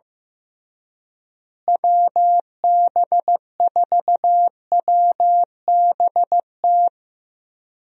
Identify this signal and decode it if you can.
Morse code, heard as a keyed tone, about 15 wpm, 700 Hz. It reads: WB4WBT
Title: USA Call Signs - 6 Characters - Repeated 3x - 15wpm